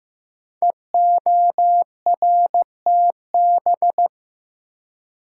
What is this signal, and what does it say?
Morse code, heard as a keyed tone, about 15 wpm, 700 Hz. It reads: EORTB
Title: Code Groups - 5 Letter:Numbers - 15wpm